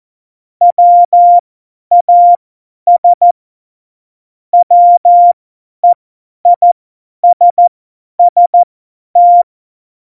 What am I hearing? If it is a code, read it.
Morse code, heard as a keyed tone, about 14 wpm, spaced as for 10 wpm, 700 Hz. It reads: WAS WEISST
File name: Morsekurs / Witze / Witz 07. Mai 2019, 50 F70